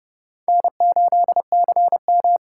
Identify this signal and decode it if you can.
Morse code, heard as a keyed tone, about 30 wpm, 700 Hz. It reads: N8CM